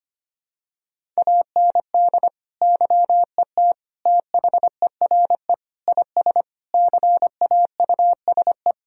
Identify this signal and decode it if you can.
Morse code, heard as a keyed tone, about 25 wpm, 700 Hz. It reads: AND YET THERE IS CAUSE
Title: Sentences from Top 1000 Words - Review Part 1 - 25wpm